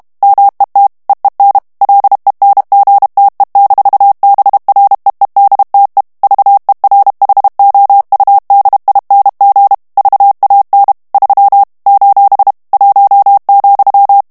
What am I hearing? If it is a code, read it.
Morse code, heard as a keyed tone, about 32 wpm, 800 Hz. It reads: MET EEN LENGTE-BREEDTE VERHOUDING VAN 3 : 1,
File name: morse mp3 / HOLLANDAIS texte clair 1 vitesse 30 mots minute